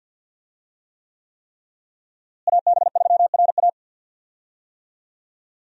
Morse code, heard as a keyed tone, about 51 wpm, 700 Hz. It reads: AB3RA
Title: USA Call Signs - 5 Characters - Repeated 3x - 50wpm